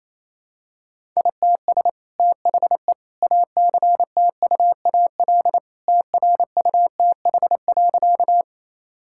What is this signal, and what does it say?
Morse code, heard as a keyed tone, about 27 wpm, 700 Hz. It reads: ITS THE ACTUAL TRUTH.
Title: Sentences from Top 1000 Words - Review Part 5 - 28wpm